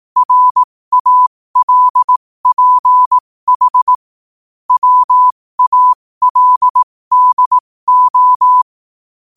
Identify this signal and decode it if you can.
Morse code, heard as a keyed tone, about 18 wpm, spaced as for 15 wpm, 1000 Hz. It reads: RALPH WALDO
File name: qotdfiles / 15wpm